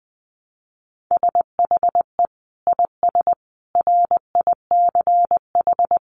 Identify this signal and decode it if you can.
Morse code, heard as a keyed tone, about 20 wpm, 700 Hz. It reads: SHE IS RICH
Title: Sentences from Top 1100 Words 20wpm